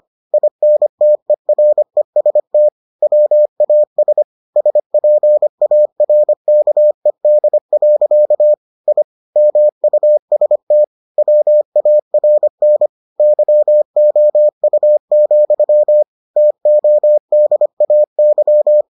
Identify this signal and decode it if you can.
Morse code, heard as a keyed tone, about 25 wpm, 600 Hz. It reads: INTEREST WAS SPARKED. I MUST WARN YOU, TODAY